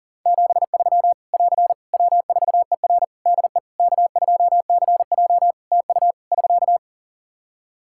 Morse code, heard as a keyed tone, about 37 wpm, 700 Hz. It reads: 73 <AR> W4ER DE K2CJ TU <SK>
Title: Practice QSO - Exchange 4 - Ending - 40wpm